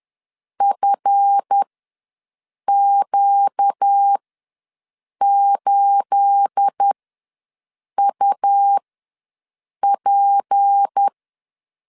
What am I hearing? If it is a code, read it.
Morse code, heard as a keyed tone, about 10 wpm, spaced as for 6 wpm, 800 Hz. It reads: FQ8UP
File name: LEZIONI / 9 28 ws 10wpm 4124